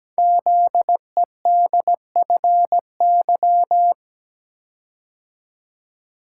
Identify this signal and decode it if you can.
Morse code, heard as a keyed tone, about 17 wpm, 700 Hz. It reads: ZEDFY
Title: Code Groups - 5 Letters - 17wpm